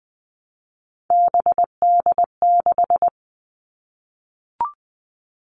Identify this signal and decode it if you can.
Morse code, heard as a keyed tone, about 20 wpm, 700 Hz. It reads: BD6
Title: All Cut Numbers - 3-Digit 20wpm